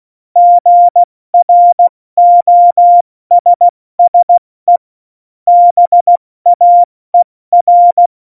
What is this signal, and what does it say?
Morse code, heard as a keyed tone, about 16 wpm, 700 Hz. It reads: GROSSE BAER
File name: Morsekurs / Witze / Witz 13. November 2018, 70 F80